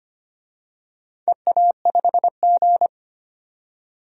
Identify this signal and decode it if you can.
Morse code, heard as a keyed tone, about 25 wpm, 700 Hz. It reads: EA5G
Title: DXCC-100 Call Signs - 4 Characters - 25wpm